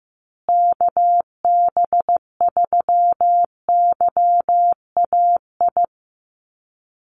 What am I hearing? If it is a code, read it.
Morse code, heard as a keyed tone, about 15 wpm, 700 Hz. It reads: KB3YAI
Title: USA Call Signs Repeated 2x 15wpm